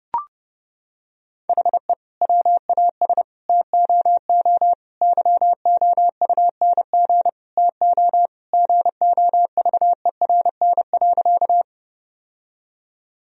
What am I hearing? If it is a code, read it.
Morse code, heard as a keyed tone, about 32 wpm, 700 Hz. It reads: HE WAS TOO YOUNG TO GOVERN.
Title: Sentences from Top 1000 Words - Review Part 2 - 30wpm